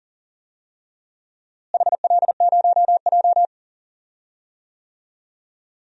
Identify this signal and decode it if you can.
Morse code, heard as a keyed tone, about 40 wpm, 700 Hz. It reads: HL0J
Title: DXCC-100 Call Signs - 4 Characters - Repeated 3x - 40wpm